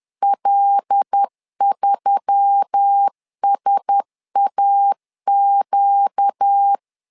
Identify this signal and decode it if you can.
Morse code, heard as a keyed tone, about 10 wpm, 800 Hz. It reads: L3SAQ
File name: LEZIONI / 5 LBVC / 5 L 10wpm 2259